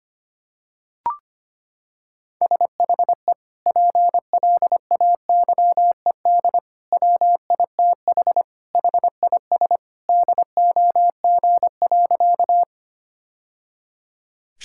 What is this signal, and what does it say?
Morse code, heard as a keyed tone, about 25 wpm, 700 Hz. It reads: SHE PLAYED WITH HIS DOG.
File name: practice / 25wpm / Sentences with 2-Word Phrases - 25wpm